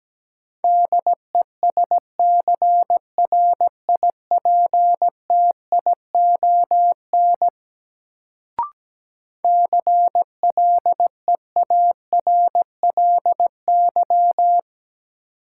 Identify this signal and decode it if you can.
Morse code, heard as a keyed tone, about 17 wpm, 700 Hz. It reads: DESCRIPTION CLEARLY
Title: Sentences from Top 1300 Words - New Word Drill Alt - 17wpm